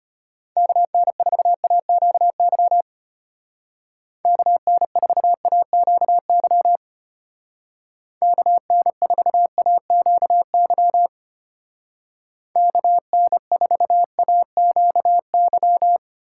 Morse code, heard as a keyed tone, about 30 wpm, 700 Hz. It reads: KN4AQY KN4AQY KN4AQY KN4AQY
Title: Speed Racing - USA Call Signs 25wpm